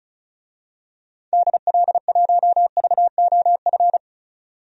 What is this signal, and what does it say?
Morse code, heard as a keyed tone, about 34 wpm, 700 Hz. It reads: DL1VOF
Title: DXCC-100 Call Signs - 6 Characters - 35wpm